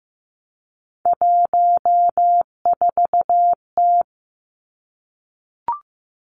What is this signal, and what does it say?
Morse code, heard as a keyed tone, about 15 wpm, 700 Hz. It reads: 14T
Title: Three Challenging Letter-Number Combinations 15wpm